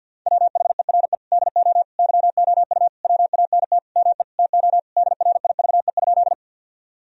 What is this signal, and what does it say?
Morse code, heard as a keyed tone, about 53 wpm, 700 Hz. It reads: WHERE DO YOU WANT ME TO DRIVE?
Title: Sentences from Top 1000 Words - Review Part 2 - 50wpm